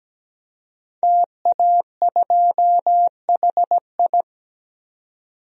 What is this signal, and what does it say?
Morse code, heard as a keyed tone, about 17 wpm, 700 Hz. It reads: TA2HI
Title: DXCC-100 Call Signs - 5 Characters - 17wpm